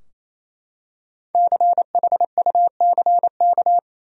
Morse code, heard as a keyed tone, about 28 wpm, 700 Hz. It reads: CHUCK
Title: USA Common Names - Max 5 Letters - 28wpm